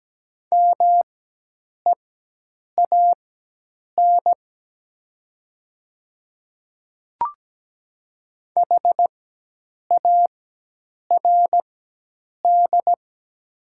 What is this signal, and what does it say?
Morse code, heard as a keyed tone, about 17 wpm, spaced as for 8 wpm, 700 Hz. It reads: MEAN HARD
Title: 4x Character Spacing - Top 300 Words - 17wpm